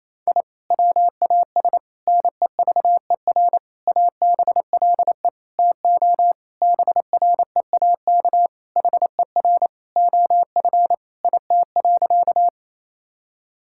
Morse code, heard as a keyed tone, about 28 wpm, 700 Hz. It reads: I WAS NEVER ABLE TO BREAK HER OF IT.
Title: Sentences from Top 1000 Words - Review Part 6 - 28wpm